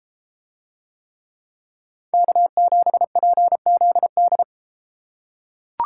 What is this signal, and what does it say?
Morse code, heard as a keyed tone, about 33 wpm, 700 Hz. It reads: K7PZD